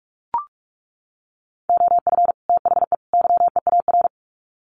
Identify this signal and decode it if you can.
Morse code, heard as a keyed tone, about 44 wpm, 700 Hz. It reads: OF THE YEAR
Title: Sets of 3 Words 45wpm